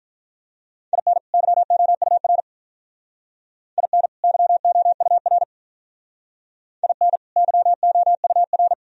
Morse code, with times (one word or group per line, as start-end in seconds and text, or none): 0.9-1.2 IN
1.3-2.4 YOUR
3.8-4.1 IN
4.2-5.4 YOUR
6.8-7.2 IN
7.4-8.7 YOUR